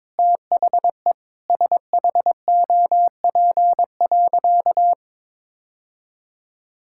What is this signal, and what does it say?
Morse code, heard as a keyed tone, about 21 wpm, 700 Hz. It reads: THE SHOP.
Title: Sentences from Top 1600 Words 22wpm